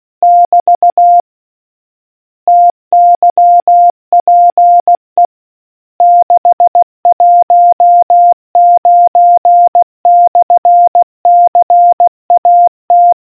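Morse code, heard as a keyed tone, about 16 wpm, 700 Hz. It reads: <BT> TYPE 619/CAT
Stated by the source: Morse quiz question from the RNZN Communicators Association.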